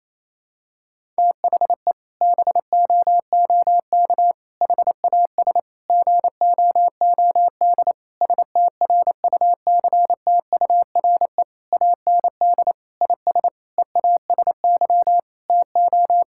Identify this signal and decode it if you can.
Morse code, heard as a keyed tone, about 28 wpm, 700 Hz. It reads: THE BOOK HAS GOOD STRUCTURE AND IS EASY TO
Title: Sentences from Top 1000 Words - Review Part 6 - 28wpm